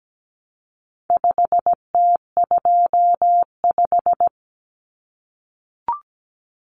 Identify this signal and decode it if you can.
Morse code, heard as a keyed tone, about 17 wpm, 700 Hz. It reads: 5T25
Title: Common Cut Numbers - 4-Digit 17wpm